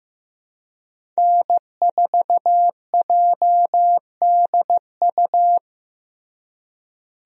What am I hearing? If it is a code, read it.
Morse code, heard as a keyed tone, about 15 wpm, 700 Hz. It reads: N4JDU
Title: USA Call Signs - 5 Characters - Repeated 2x - 15wpm